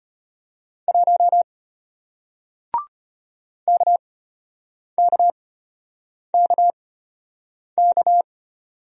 Morse code, heard as a keyed tone, about 34 wpm, spaced as for 8 wpm, 700 Hz. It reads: 1 KKKK